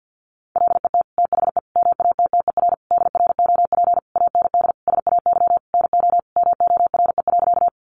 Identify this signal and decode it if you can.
Morse code, heard as a keyed tone, about 50 wpm, 700 Hz. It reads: LET THE MATTER DROP AND SAY NO MORE.
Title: Sentences from Top 1400 Words 50wpm